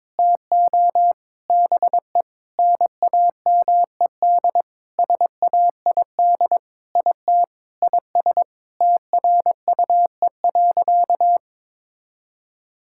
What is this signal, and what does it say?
Morse code, heard as a keyed tone, about 22 wpm, 700 Hz. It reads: TO BE NAMED SAID IT IS TRUE.
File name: Sentences with 7-Word Phrases - 22wpm